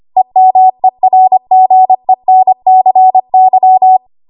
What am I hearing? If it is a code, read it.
Morse code, heard as a keyed tone, about 25 wpm, 750 Hz. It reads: EMERGENCY
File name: Archive / 25 WPM / 25WPM